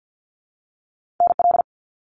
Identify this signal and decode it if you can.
Morse code, heard as a keyed tone, about 50 wpm, 700 Hz. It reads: NL
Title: Canadian Provinces and Territories - Repeated 3x - 50wpm